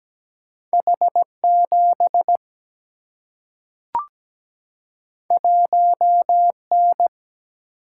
Morse code, heard as a keyed tone, about 17 wpm, 700 Hz. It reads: H 7 1 N
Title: Single-Letter Single-Number 17wpm